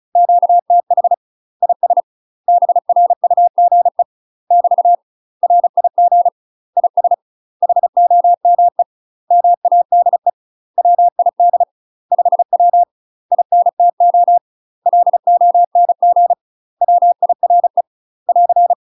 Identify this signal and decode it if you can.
Morse code, heard as a keyed tone, about 34 wpm, 700 Hz. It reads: QTH IS BRUGE <BT> RIG IS HOME MADE WID 5W INTO LONG WIRE <AR>
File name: UK MASTER QSOs TRAINING / 35WPM / UK MASTER QSOs 0039